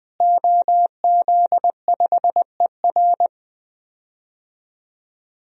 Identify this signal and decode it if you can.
Morse code, heard as a keyed tone, about 20 wpm, 700 Hz. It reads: OZ5ER
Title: DXCC-100 Call Signs - 5 Characters - Repeated 3x - 20wpm